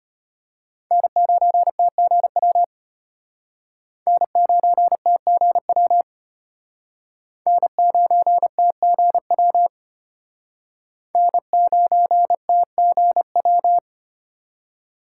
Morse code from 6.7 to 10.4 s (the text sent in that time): N9TGW